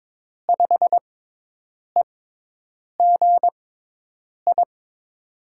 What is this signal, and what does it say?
Morse code, heard as a keyed tone, about 22 wpm, spaced as for 8 wpm, 700 Hz. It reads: 5EGI
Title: USA Call Signs - 5 Characters - 6x Character Spacing - 22wpm